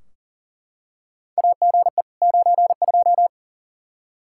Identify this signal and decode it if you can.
Morse code, heard as a keyed tone, about 40 wpm, 700 Hz. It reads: AGE 92
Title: Practice QSO - Exchange 3 - More Info - 40wpm